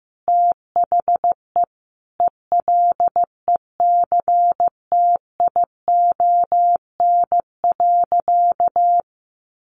Morse code, heard as a keyed tone, about 14 wpm, 700 Hz. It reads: THE ELECTION.
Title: Sentences from Top 1500 Words 15wpm